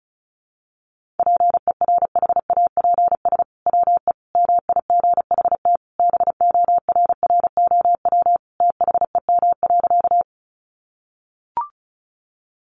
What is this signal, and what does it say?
Morse code, heard as a keyed tone, about 34 wpm, 700 Hz. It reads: PERHAPS WE MIGHT BORROW THEM.
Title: Sentences from Top 1500 Words 35wpm